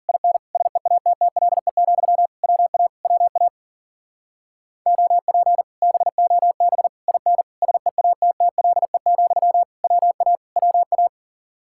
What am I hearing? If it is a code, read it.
Morse code, heard as a keyed tone, about 51 wpm, 700 Hz. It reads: IN SEATTLE, WA WA OP BOB IN SEATTLE, WA WA